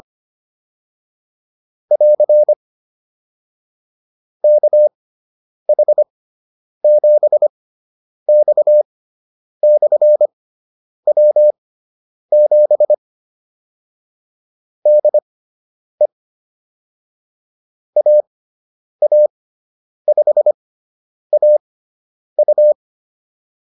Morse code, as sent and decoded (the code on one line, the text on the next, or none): .-.-. / -.- .... --... -..- -..-. .-- --... / -.. / .- .- ..... .- ..-
<AR> KH7X/W7 DE AA5AU